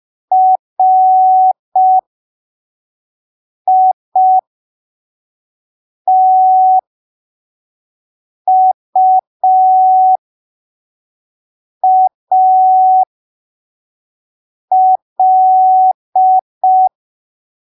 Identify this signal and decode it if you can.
Morse code, heard as a keyed tone, about 5 wpm, spaced as for 3 wpm, 750 Hz. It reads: RITUAL
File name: arquivos / texto / texto 08 05ppm